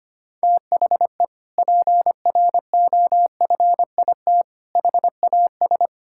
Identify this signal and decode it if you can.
Morse code, heard as a keyed tone, about 25 wpm, 700 Hz. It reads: THE PROFIT HAS
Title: Sentences from Top 1000 Words - Review Part 5 - 25wpm